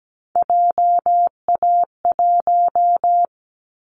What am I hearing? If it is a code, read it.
Morse code, heard as a keyed tone, about 17 wpm, 700 Hz. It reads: JA1
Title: Callsign Prefixes for Top 50 DXCC - 17wpm